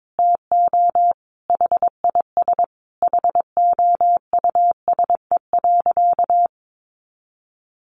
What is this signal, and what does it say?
Morse code, heard as a keyed tone, about 22 wpm, 700 Hz. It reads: TO HIS HOUSE.
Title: Additional Easy Phrases 22wpm